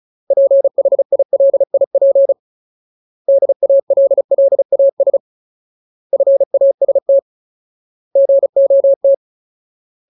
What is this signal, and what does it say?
Morse code, heard as a keyed tone, about 34 wpm, 550 Hz. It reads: PHILIP DALLAS FAST GOT
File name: Ultimate Word List / 35WPM Ultimate